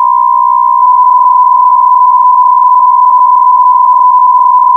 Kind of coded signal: PSK63